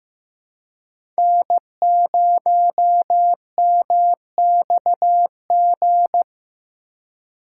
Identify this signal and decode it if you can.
Morse code, heard as a keyed tone, about 15 wpm, 700 Hz. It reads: N0MXG
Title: USA Call Signs - 4x Character Spacing - 15wpm